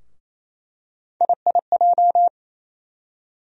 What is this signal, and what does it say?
Morse code, heard as a keyed tone, about 27 wpm, 700 Hz. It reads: IIJ
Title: USA Call Sign Postfixes - 3 Letters 28wpm